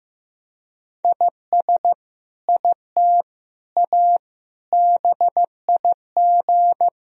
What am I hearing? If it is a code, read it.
Morse code, heard as a keyed tone, about 16 wpm, 700 Hz. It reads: IS IT A BIG
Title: Sentences from Top 1000 Words - Review Part 3 - 15wpm